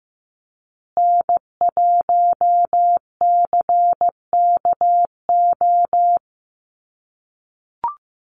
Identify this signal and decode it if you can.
Morse code, heard as a keyed tone, about 15 wpm, 700 Hz. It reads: N1CKO
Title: USA Call Signs Repeated 3x 15wpm